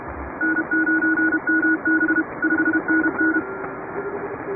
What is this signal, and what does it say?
Morse code, heard as a keyed tone, about 31 wpm, 1460 Hz. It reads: N9MB 5NN
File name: chatham / 2350-12CW